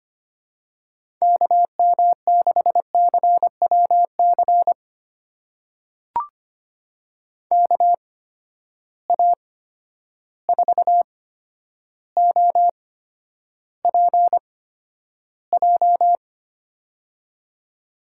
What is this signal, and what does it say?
Morse code, heard as a keyed tone, about 25 wpm, spaced as for 11 wpm, 700 Hz. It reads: KM6CWC KA4OPJ
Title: USA Call Signs - 6 Characters - 8x Character Spacing - 25wpm